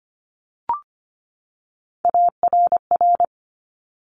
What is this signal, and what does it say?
Morse code, heard as a keyed tone, about 25 wpm, 700 Hz. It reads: ARR
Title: Top Three Letter Combinations in Words 25wpm